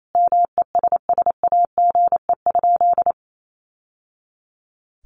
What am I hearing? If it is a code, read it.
Morse code, heard as a keyed tone, about 28 wpm, 700 Hz. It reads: MESSAGE?
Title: Sentences from Top 1100 Words 28wpm